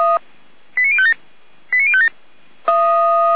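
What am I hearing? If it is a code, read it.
Five-tone selective call (ZVEI), heard as quick runs of five tones, tones of about 70 ms. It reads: 89057 89057
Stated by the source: ZVEI five-tone selective call received on an SDR.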